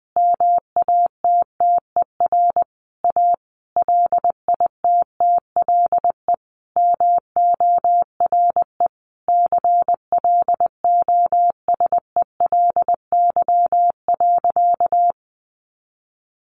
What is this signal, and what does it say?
Morse code, heard as a keyed tone, about 20 wpm, 700 Hz. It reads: MATTER A LITTLE MORE CLOSELY.
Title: Sentences from Top 1400 Words 20wpm